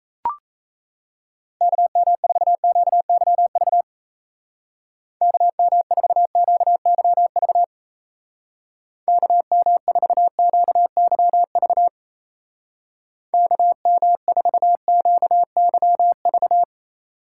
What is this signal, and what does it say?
Morse code, heard as a keyed tone, about 34 wpm, 700 Hz. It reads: KM4QYV KM4QYV KM4QYV KM4QYV
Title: Speed Racing - USA 2x3 Call Signs 28wpm